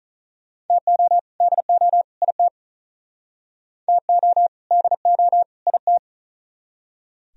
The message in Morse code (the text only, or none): TO DO IT TO DO IT